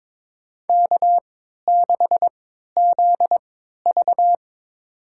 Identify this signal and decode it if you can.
Morse code, heard as a keyed tone, about 22 wpm, spaced as for 13 wpm, 700 Hz. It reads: K6ZV
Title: USA Call Signs - 4 Characters - 3x Character Spacing - 22wpm